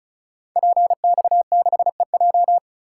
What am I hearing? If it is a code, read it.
Morse code, heard as a keyed tone, about 36 wpm, 690 Hz. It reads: PX6EJ
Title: Code Groups - 5 Letter:Numbers - 35wpm